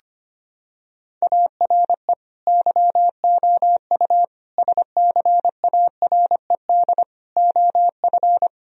Morse code, heard as a keyed tone, about 25 wpm, 690 Hz. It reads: ARE YOU SCARED OF
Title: Sentences from Top 1000 Words - Review Part 4 - 25wpm